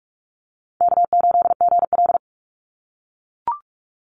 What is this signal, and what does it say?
Morse code, heard as a keyed tone, about 44 wpm, 700 Hz. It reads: K8GL
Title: Easy Call Signs - 45wpm